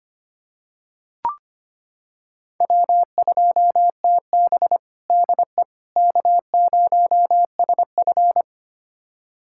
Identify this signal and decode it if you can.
Morse code, heard as a keyed tone, about 25 wpm, 700 Hz. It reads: W2TB DE K0SF
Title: USA Callsign DE Callsign - 4 Characters - 25wpm